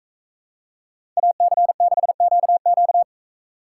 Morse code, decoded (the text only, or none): AC6QQ